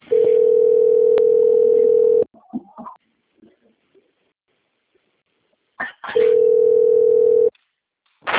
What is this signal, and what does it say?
Ringback tone (North America), 440 + 480 Hz.